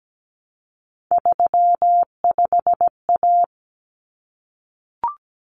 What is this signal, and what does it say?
Morse code, heard as a keyed tone, about 17 wpm, 690 Hz. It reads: 35A
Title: Common Cut Numbers - 3-Digit 17wpm